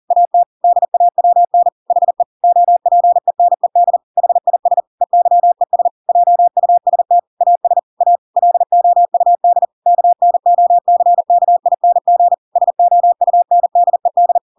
Morse code, heard as a keyed tone, about 40 wpm, 700 Hz. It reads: AT DAWN HE OPENED HIS EYES JUST AS A LOUD KNOCKING SOUNDED
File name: Pinocchio / Pinocchio ch6 40 40 ik1hge